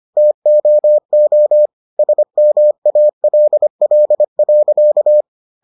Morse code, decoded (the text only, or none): TOO SMALL.